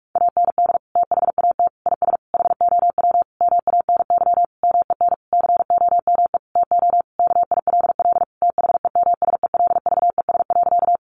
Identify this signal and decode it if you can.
Morse code, heard as a keyed tone, about 44 wpm, 700 Hz. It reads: AND THAT IS HOW MANY MEN COME TO KILL THEMSELVES.
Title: Sentences from Top 1400 Words 45wpm